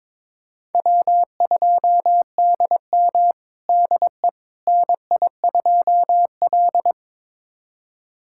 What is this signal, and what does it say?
Morse code, heard as a keyed tone, about 22 wpm, 700 Hz. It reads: W2DM DE NI2L
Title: USA Callsign DE Callsign - 4 Characters - 22wpm